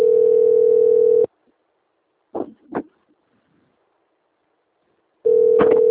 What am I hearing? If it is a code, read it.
Ringback tone (North America), 440 + 480 Hz.
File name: out- - - - - .3743